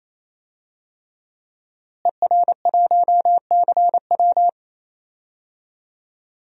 Morse code, heard as a keyed tone, about 28 wpm, 700 Hz. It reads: ER1CW
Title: DXCC-100 Call Signs - 5 Characters - Repeated 3x - 28wpm